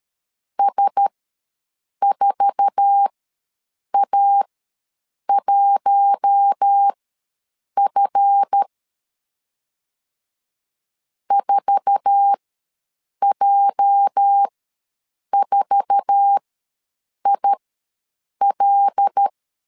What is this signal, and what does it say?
Morse code, heard as a keyed tone, about 13 wpm, spaced as for 7 wpm, 800 Hz. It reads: S4A1F 4J4IL
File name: LEZIONI / 8 YJ45 / 8 YJ4 ws 12wpm 3249